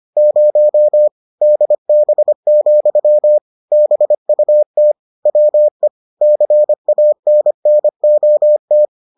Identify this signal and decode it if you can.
Morse code, heard as a keyed tone, about 25 wpm, 600 Hz. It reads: 0 DB, BUT WE CANNOT